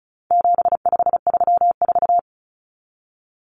Morse code, heard as a keyed tone, about 34 wpm, 700 Hz. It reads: 7534
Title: Common Cut Numbers - 4-Digit 35wpm